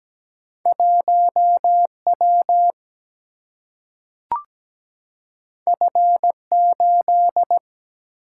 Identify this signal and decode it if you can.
Morse code, heard as a keyed tone, about 17 wpm, 700 Hz. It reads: 1W F8
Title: Single-Letter Single-Number 17wpm